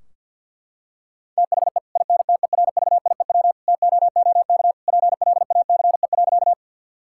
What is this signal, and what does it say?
Morse code, heard as a keyed tone, about 50 wpm, 700 Hz. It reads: THE INTERVIEW TOOK PLACE.
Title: Sentences from Top 1600 Words 50wpm